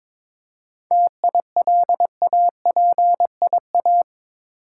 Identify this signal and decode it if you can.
Morse code, heard as a keyed tone, about 22 wpm, 700 Hz. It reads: TILAPIA
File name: Difficult Words with Hints - 22wpm